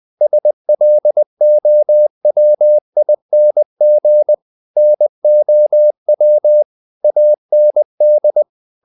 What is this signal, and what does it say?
Morse code, heard as a keyed tone, about 20 wpm, 600 Hz. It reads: SLOWING NOW AND
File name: the old man and the sea part 2 20wpm